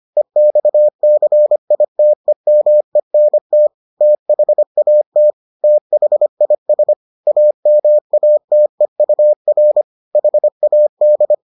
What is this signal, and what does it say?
Morse code, heard as a keyed tone, about 25 wpm, 600 Hz. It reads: EXCITEMENT THAT THIS AMATEUR HAD